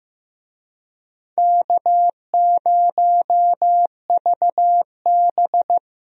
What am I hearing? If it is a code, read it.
Morse code, heard as a keyed tone, about 15 wpm, 700 Hz. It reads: K0VB
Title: USA Call Signs - 4 Characters - 15wpm